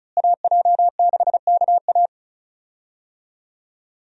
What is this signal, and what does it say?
Morse code, heard as a keyed tone, about 36 wpm, 700 Hz. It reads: AJ6KA
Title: USA Call Signs - 5 Characters - Repeated 2x - 35wpm